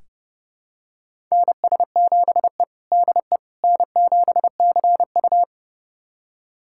Morse code, heard as a keyed tone, about 32 wpm, 700 Hz. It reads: NS7E DE N7CU
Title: USA Callsign DE Callsign - 4 Characters - 30wpm